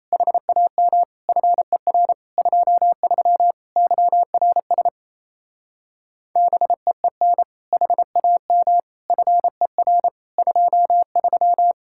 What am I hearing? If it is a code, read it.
Morse code, heard as a keyed tone, about 38 wpm, 700 Hz. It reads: HAM FER 23 YRS BEEN HAM FER 23